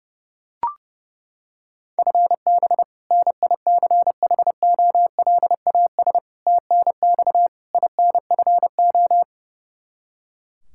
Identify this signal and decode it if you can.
Morse code, heard as a keyed tone, about 30 wpm, 700 Hz. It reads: FB NICHOLAS TNX INFO
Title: K1USN SST QSO Practice 30wpm